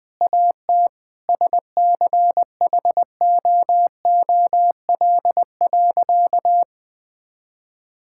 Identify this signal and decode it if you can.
Morse code, heard as a keyed tone, about 20 wpm, 700 Hz. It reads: AT SCHOOL.
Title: Sentences from Top 1000 Words - Review Part 6 - 20wpm